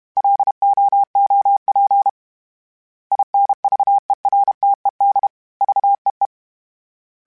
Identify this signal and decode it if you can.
Morse code, heard as a keyed tone, about 32 wpm, 800 Hz. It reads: LOOP INVERTED VEE
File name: fromSMs / FromSMs mp3s / Antennas@30